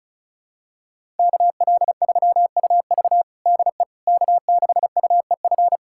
Morse code, heard as a keyed tone, about 36 wpm, 700 Hz. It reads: KL3UV DE K6UEF
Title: USA Callsign DE Callsign - 5 Characters - 35wpm